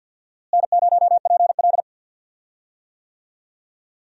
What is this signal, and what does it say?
Morse code, heard as a keyed tone, about 48 wpm, 700 Hz. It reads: N0PL
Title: USA Call Signs - 4 Characters - Repeated 2x - 50wpm